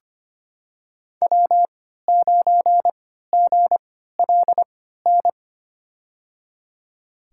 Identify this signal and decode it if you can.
Morse code, heard as a keyed tone, about 25 wpm, spaced as for 14 wpm, 700 Hz. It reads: W9GLN
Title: USA Call Signs - 5 Characters - 3x Character Spacing - 25wpm